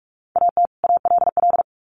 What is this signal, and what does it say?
Morse code, heard as a keyed tone, about 44 wpm, 700 Hz. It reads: AT ALL
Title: Sets of 2 Words 45wpm